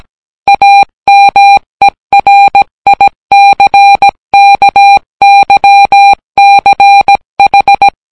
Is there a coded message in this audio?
Yes: Morse code from 0.5 to 7.9 s, heard as a keyed tone, about 17 wpm, 800 Hz. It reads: AMERICKYCH